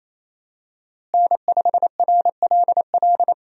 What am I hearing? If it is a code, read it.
Morse code, heard as a keyed tone, about 28 wpm, 700 Hz. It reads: N5RLL
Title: USA Call Signs - 5 Characters - Repeated 2x - 28wpm